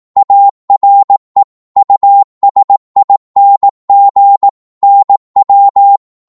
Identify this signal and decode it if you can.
Morse code, heard as a keyed tone, about 18 wpm, 800 Hz. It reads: ARE USING NW
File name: Morse / qso-18wpm